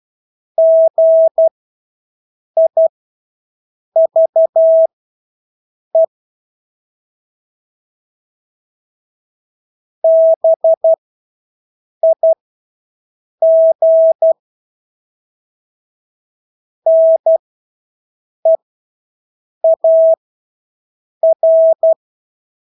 Morse code, heard as a keyed tone, about 12 wpm, spaced as for 6 wpm, 650 Hz. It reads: GIVE BIG NEAR
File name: Engelske ord 12-6 wpm